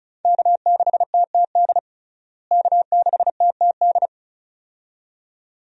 Morse code, heard as a keyed tone, about 36 wpm, 700 Hz. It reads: K6TTD K6TTD
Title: USA Call Signs - 5 Characters - Repeated 2x - 35wpm